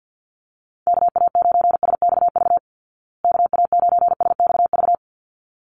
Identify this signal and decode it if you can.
Morse code, heard as a keyed tone, about 50 wpm, 700 Hz. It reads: KA9SXV KA9SXV
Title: Call Signs Repeated 3x - 50wpm